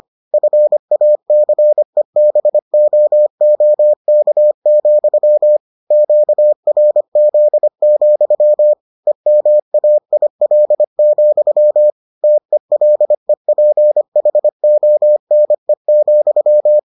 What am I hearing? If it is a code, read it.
Morse code, heard as a keyed tone, about 25 wpm, 600 Hz. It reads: FACEBOOK, QRZ, EMAIL, TELEPHONE,